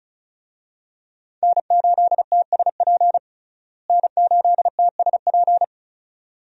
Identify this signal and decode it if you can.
Morse code, heard as a keyed tone, about 34 wpm, 700 Hz. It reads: N8TSP N8TSP
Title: USA Call Signs - 5 Characters - Repeated 2x - 35wpm